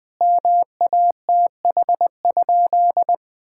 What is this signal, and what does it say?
Morse code, heard as a keyed tone, about 20 wpm, 700 Hz. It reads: MATH?
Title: Sentences from Top 1000 Words - Review Part 6 - 20wpm